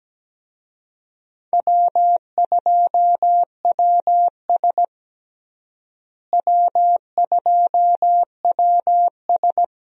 Morse code, heard as a keyed tone, about 17 wpm, 700 Hz. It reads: W2WS W2WS
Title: USA Call Signs - 4 Characters - Repeated 3x - 17wpm